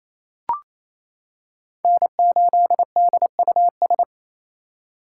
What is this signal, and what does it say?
Morse code, heard as a keyed tone, about 28 wpm, 700 Hz. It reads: N8DUS
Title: K1USN SST QSO Practice - 28wpm